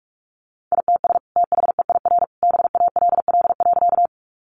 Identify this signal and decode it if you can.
Morse code, heard as a keyed tone, about 44 wpm, 700 Hz. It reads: ITS THEIR BALL.
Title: Sentences from Top 900 Words 45wpm